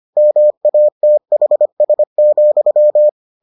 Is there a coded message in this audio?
Yes: Morse code from 0.2 to 3.1 s, heard as a keyed tone, about 25 wpm, 600 Hz. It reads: MATHS,